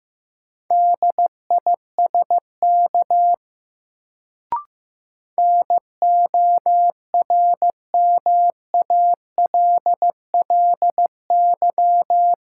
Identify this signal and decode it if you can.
Morse code, heard as a keyed tone, about 15 wpm, 700 Hz. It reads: DISK NORMALLY
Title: Sentences from Top 1000 Words - Review Part 6 - New Word Drill - 15wpm